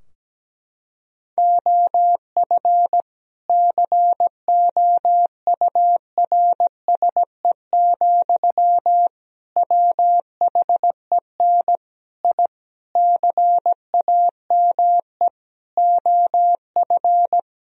Morse code, heard as a keyed tone, about 17 wpm, 700 Hz. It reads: OF COURSE, WHEN I CAME OF
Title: Sentences from Top 1000 Words - Review Part 3 - 17wpm